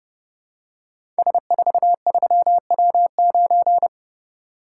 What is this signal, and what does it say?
Morse code, heard as a keyed tone, about 29 wpm, 700 Hz. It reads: S43W9
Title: Code Groups - 5 Letter:Numbers - 30wpm